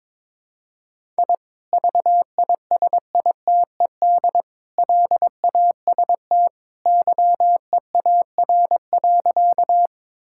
Morse code, heard as a keyed tone, about 22 wpm, 700 Hz. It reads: I VISITED LAST YEAR.
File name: practice / 22wpm / Sentences with 2-Word Phrases - 22wpm